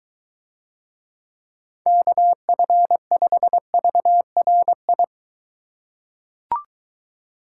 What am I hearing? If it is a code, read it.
Morse code, heard as a keyed tone, about 23 wpm, 700 Hz. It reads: KF5VRI